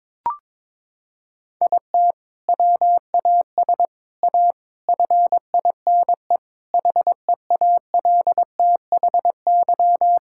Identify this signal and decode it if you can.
Morse code, heard as a keyed tone, about 22 wpm, 700 Hz. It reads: IT WAS A FINE HEALTHY
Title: Sentences from Top 1000 Words - Review Part 5 - 22wpm